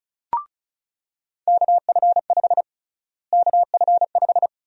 Morse code, heard as a keyed tone, about 36 wpm, 700 Hz. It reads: KF5 KF5
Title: USA Call Sign Prefixes - Repeated 2x - 35wpm